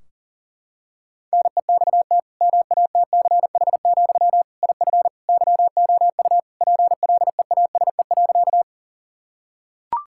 Morse code, heard as a keyed tone, about 40 wpm, 700 Hz. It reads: NEXT MATCH, IF YOU PLEASE.
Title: Sentences from Top 1600 Words 40wpm